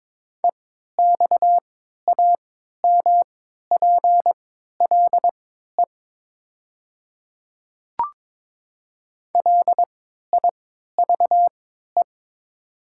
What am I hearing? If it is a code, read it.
Morse code, heard as a keyed tone, about 22 wpm, spaced as for 13 wpm, 700 Hz. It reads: EXAMPLE LIVE
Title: 3x Character Spacing - Top 300 Words - 22wpm